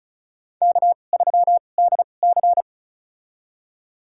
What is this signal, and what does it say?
Morse code, heard as a keyed tone, about 34 wpm, spaced as for 25 wpm, 700 Hz. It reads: K3DC
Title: USA Call Signs - 4 Characters - 2x Character Spacing - 35wpm